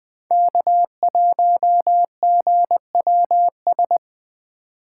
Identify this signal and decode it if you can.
Morse code, heard as a keyed tone, about 24 wpm, 700 Hz. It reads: K1GWS
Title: Speed Racing - USA 1x3 Call Signs 20wpm